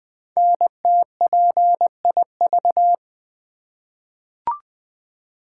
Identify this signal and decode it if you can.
Morse code, heard as a keyed tone, about 20 wpm, 700 Hz. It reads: NTPIV
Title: Code Groups - 5 Letter:Numbers - 20wpm